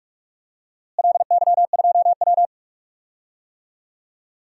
Morse code, heard as a keyed tone, about 46 wpm, 700 Hz. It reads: LY2W
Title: DXCC-100 Call Signs - 4 Characters - Repeated 2x - 45wpm